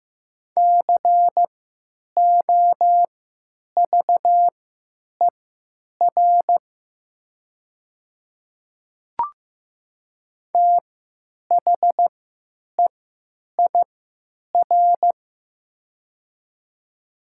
Morse code, heard as a keyed tone, about 15 wpm, spaced as for 8 wpm, 700 Hz. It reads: COVER THEIR
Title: 3x Character Spacing - Top 300 Words - 15wpm